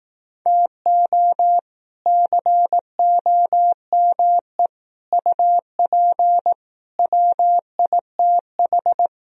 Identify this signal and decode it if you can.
Morse code, heard as a keyed tone, about 18 wpm, 700 Hz. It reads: TO COME UP WITH
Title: Speed-Racing - Sets of 4 Words 15wpm